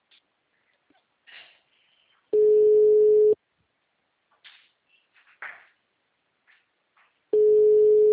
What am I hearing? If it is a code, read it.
Ringback tone (Europe, 425 Hz).